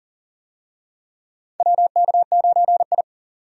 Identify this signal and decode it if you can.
Morse code, heard as a keyed tone, about 40 wpm, 700 Hz. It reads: WK9I
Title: USA Call Signs - 4 Characters - 40wpm